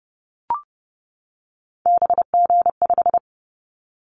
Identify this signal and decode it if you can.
Morse code, heard as a keyed tone, about 30 wpm, 700 Hz. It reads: BG5
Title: Callsign Prefixes for Top 100 DXCC - 30wpm